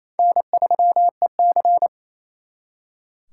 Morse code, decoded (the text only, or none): N3EC